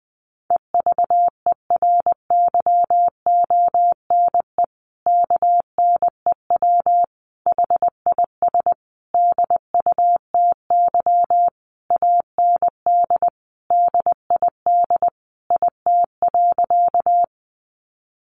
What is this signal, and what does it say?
Morse code, heard as a keyed tone, about 20 wpm, 700 Hz. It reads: EVERYONE KNEW HIS DUTY AND DID IT.
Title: Sentences from Top 1500 Words 20wpm